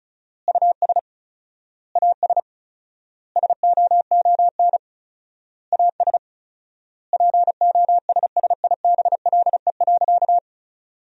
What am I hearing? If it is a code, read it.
Morse code, heard as a keyed tone, about 34 wpm, 700 Hz. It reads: US AS SOON AS POSSIBLE.
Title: 4x Word-Spacing - Sentences from Top 500 Words - 35wpm